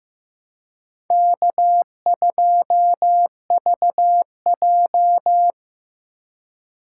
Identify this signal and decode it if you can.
Morse code, heard as a keyed tone, about 18 wpm, 700 Hz. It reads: K2VJ
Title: Speed Racing - USA 1x2 Call Signs 15wpm